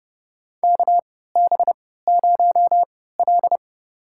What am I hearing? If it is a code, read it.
Morse code, heard as a keyed tone, about 29 wpm, spaced as for 17 wpm, 700 Hz. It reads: KB0L
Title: USA Call Signs - 4 Characters - 3x Character Spacing - 30wpm